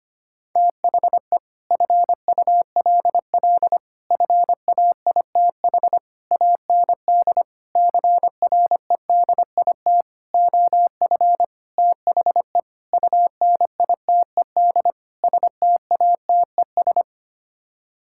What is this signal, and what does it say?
Morse code, heard as a keyed tone, about 25 wpm, 700 Hz. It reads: THE FULL FAITH AND CREDIT OF THE UNITED STATES
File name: Sets of 9 Words - 25wpm